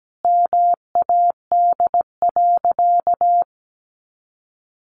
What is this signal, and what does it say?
Morse code, heard as a keyed tone, about 17 wpm, 700 Hz. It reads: MAD.
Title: Additional Easy Phrases 17wpm